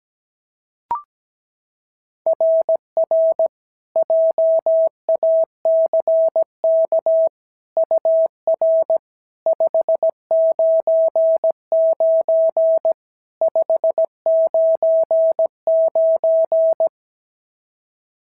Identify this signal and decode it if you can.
Morse code, heard as a keyed tone, about 16 wpm, 650 Hz. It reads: RR JACK UR 599 599
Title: SOTA QSO Practice 17wpm